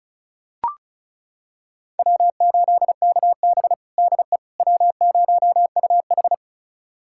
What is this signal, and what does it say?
Morse code, heard as a keyed tone, about 34 wpm, 700 Hz. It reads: W8KB DE W0UH
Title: USA Callsign DE Callsign - 4 Characters - 35wpm